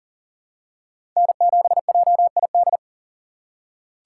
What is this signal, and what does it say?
Morse code, heard as a keyed tone, about 40 wpm, 700 Hz. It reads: N7JID
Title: USA Call Signs - 2x Character Spacing - 40wpm